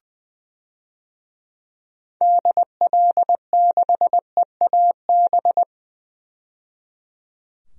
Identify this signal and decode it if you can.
Morse code, heard as a keyed tone, about 21 wpm, 700 Hz. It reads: DL6EAB
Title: DXCC-100 Call Signs - 6 Characters - 20wpm